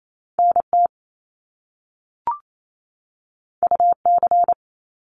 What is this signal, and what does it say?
Morse code, heard as a keyed tone, about 28 wpm, 700 Hz. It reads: NT UC
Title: Top Two Letter Combinations in Words 28wpm